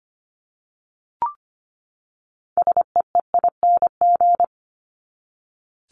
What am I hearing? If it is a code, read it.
Morse code, heard as a keyed tone, about 25 wpm, 700 Hz. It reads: SEEING
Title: ING Suffix - Top 100 Words with Base Words 25wpm